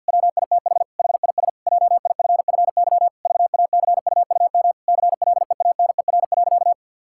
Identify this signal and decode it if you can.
Morse code, heard as a keyed tone, about 50 wpm, 700 Hz. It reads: WITH HIS JIFFY VACUUM CLEANER.
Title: Pangram Challenge v2 - 50wpm